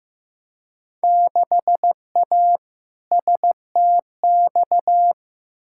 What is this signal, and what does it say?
Morse code, heard as a keyed tone, about 16 wpm, 700 Hz. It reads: 6A STX
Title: Field Day Exchanges - 15wpm